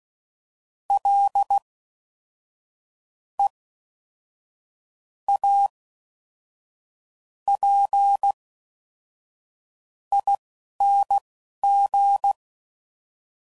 Morse code, heard as a keyed tone, about 16 wpm, spaced as for 11 wpm, 800 Hz. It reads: L E A P ING
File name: Suffixes / Special INGs / Fast INGs